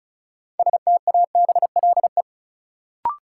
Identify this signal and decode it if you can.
Morse code, heard as a keyed tone, about 36 wpm, 700 Hz. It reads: STABLE
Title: Sentences from Top 1500 Words - New Word Drill - 35wpm